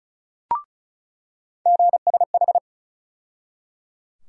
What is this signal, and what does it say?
Morse code, heard as a keyed tone, about 34 wpm, 700 Hz. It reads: GSH
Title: USA Call Sign Postfixes - 3 Letters 35wpm